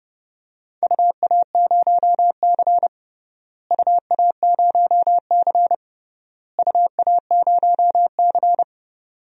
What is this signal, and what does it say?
Morse code, heard as a keyed tone, about 32 wpm, 700 Hz. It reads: UA0C UA0C UA0C
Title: DXCC-100 Call Signs - 4 Characters - Repeated 3x - 30wpm